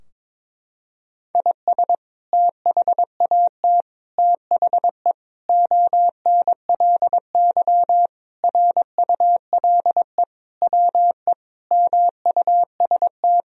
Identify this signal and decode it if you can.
Morse code, heard as a keyed tone, about 22 wpm, 700 Hz. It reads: IS THAT THE ONLY RULE WE MUST
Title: Sentences from Top 1000 Words - Review Part 2 - 22wpm